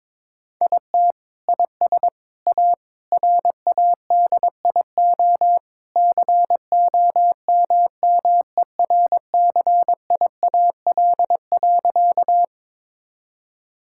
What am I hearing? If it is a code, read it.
Morse code, heard as a keyed tone, about 22 wpm, 700 Hz. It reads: IT IS A RADIO COMMERCIAL.
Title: Sentences from Top 1000 Words - Review Part 5 - 22wpm